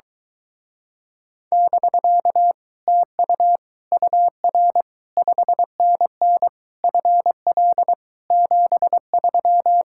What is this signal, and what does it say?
Morse code, heard as a keyed tone, about 22 wpm, 700 Hz. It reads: <BK> TU UR 5NN FL 73